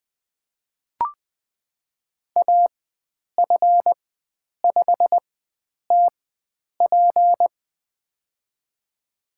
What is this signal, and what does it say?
Morse code, heard as a keyed tone, about 20 wpm, spaced as for 9 wpm, 700 Hz. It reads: AF5TP
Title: USA Call Signs - 5 Characters - 4x Character Spacing - 20wpm